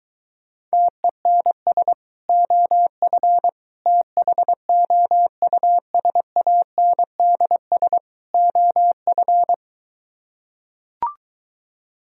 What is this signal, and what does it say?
Morse code, heard as a keyed tone, about 24 wpm, 700 Hz. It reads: TENS OF THOUSANDS OF